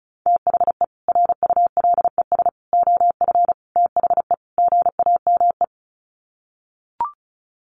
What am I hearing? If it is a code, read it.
Morse code, heard as a keyed tone, about 34 wpm, 700 Hz. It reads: THE RULES OF THE GAME
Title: Sets of 5 Words Encore2 35wpm